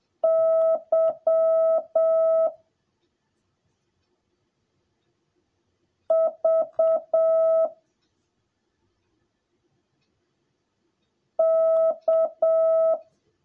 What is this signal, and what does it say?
Morse code, heard as a keyed tone, about 7 wpm, 650 Hz. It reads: Y V K
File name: jh8cbh / 2021cw / cw10a